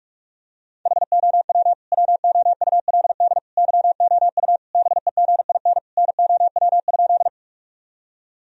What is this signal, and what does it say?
Morse code, heard as a keyed tone, about 41 wpm, 700 Hz. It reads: HOW WOULD YOU BEGIN NOW?
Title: Sentences from Top 1000 Words - Review Part 1 - 45wpm